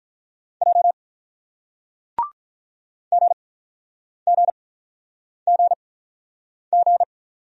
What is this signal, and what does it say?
Morse code, heard as a keyed tone, about 48 wpm, spaced as for 9 wpm, 700 Hz. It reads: J GGGG